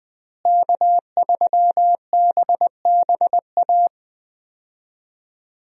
Morse code, heard as a keyed tone, about 20 wpm, 700 Hz. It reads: K3BBA
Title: USA Call Signs - 5 Characters - Repeated 3x - 20wpm